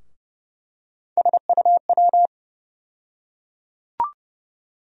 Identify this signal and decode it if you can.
Morse code, heard as a keyed tone, about 30 wpm, 700 Hz. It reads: SUW
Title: USA Call Sign Postfixes - 3 Letters 30wpm